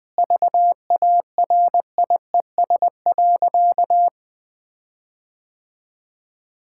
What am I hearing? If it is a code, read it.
Morse code, heard as a keyed tone, about 20 wpm, 700 Hz. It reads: VARIES.
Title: Sentences with 5-Word Phrases - 20wpm